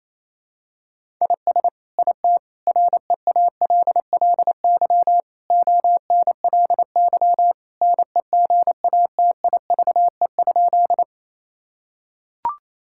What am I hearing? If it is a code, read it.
Morse code, heard as a keyed tone, about 28 wpm, 700 Hz. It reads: IS IT REALLY ONLY NEGATIVE?
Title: Sentences from Top 1000 Words - Review Part 6 - 28wpm